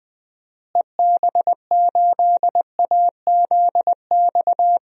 Morse code, heard as a keyed tone, about 21 wpm, 700 Hz. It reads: EB8AZX
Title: DXCC-100 Call Signs - 6 Characters - Repeated 2x - 20wpm